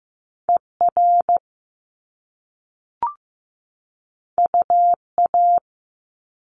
Morse code, heard as a keyed tone, about 15 wpm, 700 Hz. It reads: ER UA
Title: Top Two Letter Combinations in Words 15wpm